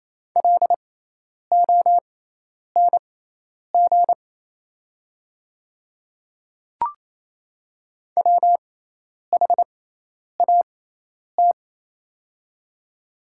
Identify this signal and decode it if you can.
Morse code, heard as a keyed tone, about 28 wpm, spaced as for 10 wpm, 690 Hz. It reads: LONG WHAT
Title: 6x Character Spacing - Top 100 Words - 28wpm